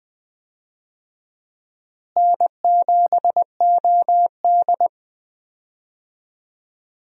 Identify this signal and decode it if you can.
Morse code, heard as a keyed tone, about 20 wpm, 700 Hz. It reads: N7OD
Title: USA Call Signs - 4 Characters - Repeated 3x - 20wpm